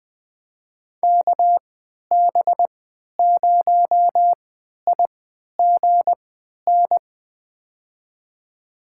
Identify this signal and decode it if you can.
Morse code, heard as a keyed tone, about 20 wpm, spaced as for 11 wpm, 700 Hz. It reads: KB0IGN
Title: USA Call Signs - 3x Character Spacing - 20wpm